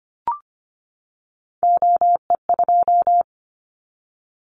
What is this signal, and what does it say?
Morse code, heard as a keyed tone, about 25 wpm, 700 Hz. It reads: OE2
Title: Callsign Prefixes for Top 100 DXCC - 25wpm